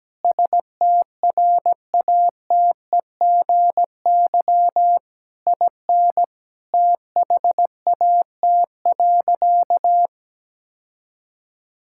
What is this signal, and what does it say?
Morse code, heard as a keyed tone, about 17 wpm, 700 Hz. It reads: STRATEGY IN THAT.
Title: Sentences from Top 1000 Words - Review Part 5 - 17wpm